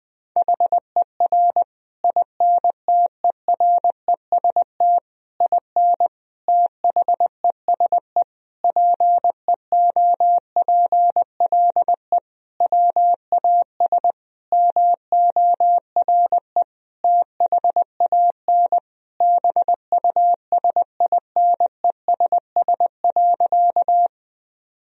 Morse code, heard as a keyed tone, about 20 wpm, 700 Hz. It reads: HER INTEREST IN THESE PEOPLE WAS MORE THAN BUSINESS.
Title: Sentences from Top 1000 Words - Review Part 2 - 20wpm